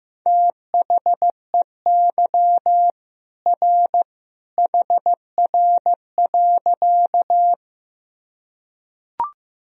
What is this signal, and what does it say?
Morse code, heard as a keyed tone, about 15 wpm, 700 Hz. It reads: THEY R HR.
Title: Sentences from Top 200 Words and Abbreviations 15wpm